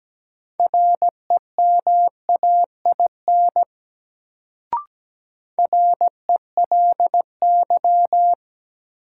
Morse code, heard as a keyed tone, about 17 wpm, 700 Hz. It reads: REMAIN RELY
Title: Sentences from Top 1200 Words - New Word Drill - 17wpm